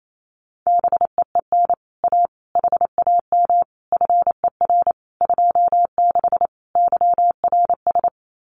Morse code, heard as a keyed tone, about 28 wpm, 700 Hz. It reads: BEEN A HAM FER 26 YRS
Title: Common QSO Phrases Encore 28wpm